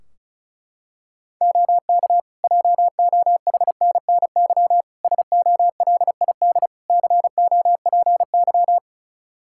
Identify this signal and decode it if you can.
Morse code, heard as a keyed tone, about 34 wpm, 700 Hz. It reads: OK JOHNNY SOLID COPY
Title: Practice QSO - Exchange 3 - More Info - 35wpm